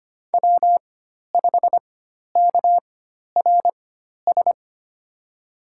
Morse code, heard as a keyed tone, about 25 wpm, spaced as for 12 wpm, 700 Hz. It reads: W5KRS
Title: USA Call Signs - 4x Character Spacing - 25wpm